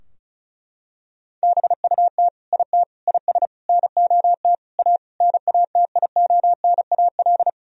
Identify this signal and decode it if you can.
Morse code, heard as a keyed tone, about 36 wpm, 700 Hz. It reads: BUT IT IS NOT A NATIONAL
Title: Sentences from Top 1000 Words - Review Part 4 - 35wpm